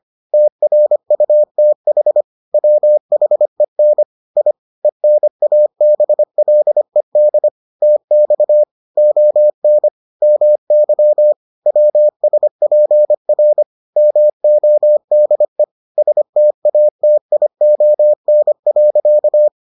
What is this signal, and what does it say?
Morse code, heard as a keyed tone, about 25 wpm, 600 Hz. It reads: TRUTH WHEN I ENABLED TX ON MY WSPR MODE STATION.